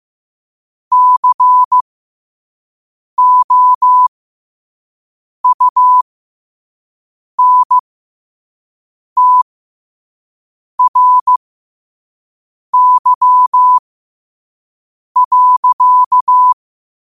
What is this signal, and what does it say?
Morse code, heard as a keyed tone, about 15 wpm, spaced as for 5 wpm, 1000 Hz. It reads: COUNTRY.
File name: qotdfiles / 5wpm